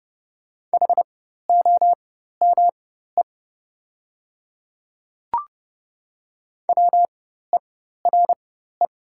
Morse code, heard as a keyed tone, about 30 wpm, spaced as for 14 wpm, 690 Hz. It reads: HOME WERE
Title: 4x Character Spacing - Top 200 Words - 30wpm